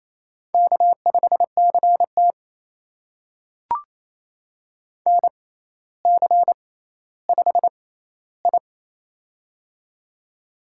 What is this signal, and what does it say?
Morse code, heard as a keyed tone, about 28 wpm, spaced as for 14 wpm, 700 Hz. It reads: K5CT NC5I
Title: USA Call Signs - 4 Characters - 6x Character Spacing - 28wpm